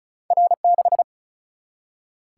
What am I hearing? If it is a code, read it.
Morse code, heard as a keyed tone, about 35 wpm, 700 Hz. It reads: R 6
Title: Single-Letter Single-Number 35wpm